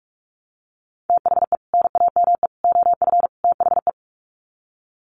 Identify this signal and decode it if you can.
Morse code, heard as a keyed tone, about 44 wpm, 700 Hz. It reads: THE NAME OF THE
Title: Sets of 4 Words 45wpm